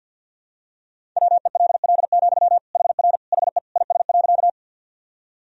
Morse code, heard as a keyed tone, about 46 wpm, 700 Hz. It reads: WELL, HR HE IS.
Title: Sentences from Top 200 Words and Abbreviations 50wpm